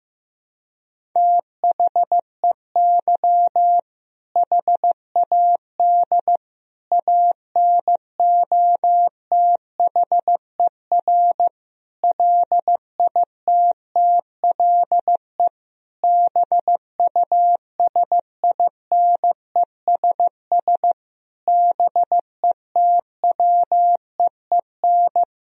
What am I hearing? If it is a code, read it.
Morse code, heard as a keyed tone, about 15 wpm, 700 Hz. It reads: THEY HAD ANOTHER LITTLE BUSINESS BETWEEN
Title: Sentences from Top 1000 Words - Review Part 1 - 15wpm